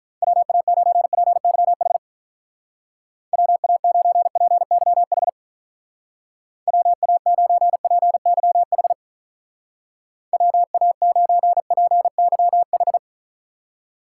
Morse code, heard as a keyed tone, about 46 wpm, 700 Hz. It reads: WA9PYH WA9PYH WA9PYH WA9PYH